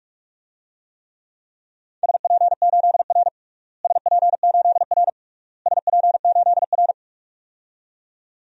Morse code, heard as a keyed tone, about 42 wpm, 700 Hz. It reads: SP8R SP8R SP8R
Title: DXCC-100 Call Signs - 4 Characters - Repeated 3x - 45wpm